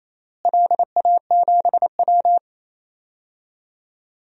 Morse code, heard as a keyed tone, about 27 wpm, 700 Hz. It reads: LA7W
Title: DXCC-100 Call Signs - 4 Characters - Repeated 3x - 28wpm